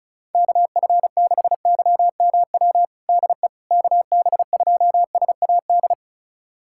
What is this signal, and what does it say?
Morse code, heard as a keyed tone, about 36 wpm, 700 Hz. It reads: KF6YMW DE KB2SAD
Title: USA Callsign DE Callsign 35wpm